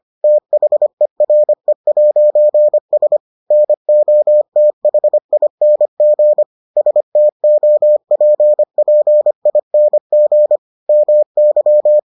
Morse code, heard as a keyed tone, about 25 wpm, 600 Hz. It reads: THERE'S NOTHING STOPPING MY